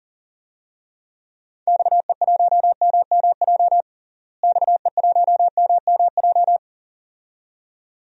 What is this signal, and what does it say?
Morse code, heard as a keyed tone, about 40 wpm, 700 Hz. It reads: XE1MMJ XE1MMJ
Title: DXCC-100 Call Signs - 6 Characters - Repeated 2x - 40wpm